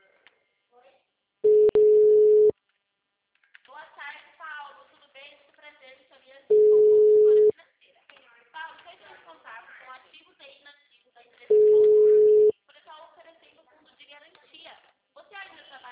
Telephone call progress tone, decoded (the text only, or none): ringback tone